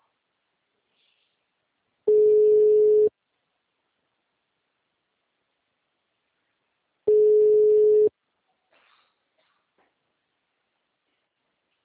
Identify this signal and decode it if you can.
Ringback tone (Europe, 425 Hz).